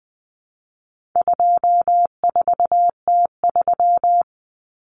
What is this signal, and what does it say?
Morse code, heard as a keyed tone, about 20 wpm, 700 Hz. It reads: 24T3
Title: Common Cut Numbers - 4-Digit 20wpm